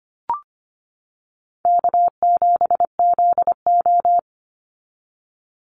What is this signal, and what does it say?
Morse code, heard as a keyed tone, about 25 wpm, 700 Hz. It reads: K7ZO
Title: NA SOTA Chaser Callsigns 25wpm